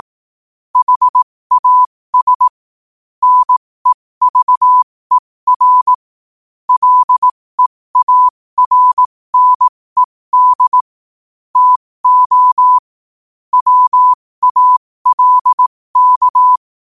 Morse code, heard as a keyed tone, about 18 wpm, spaced as for 15 wpm, 1000 Hz. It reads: HAS NEVER LEARNED TO WALK
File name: qotdfiles / 15wpm